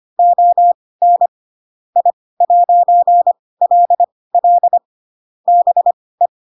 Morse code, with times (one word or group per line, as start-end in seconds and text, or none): 0.2-1.3 ON
2.0-4.8 I'LL
5.5-6.3 BE